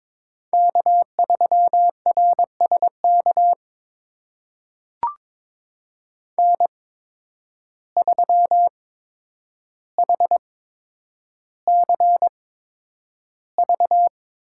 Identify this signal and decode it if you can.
Morse code, heard as a keyed tone, about 21 wpm, spaced as for 9 wpm, 700 Hz. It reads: K3RSK N3HCV
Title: USA Call Signs - 5 Characters - 8x Character Spacing - 22wpm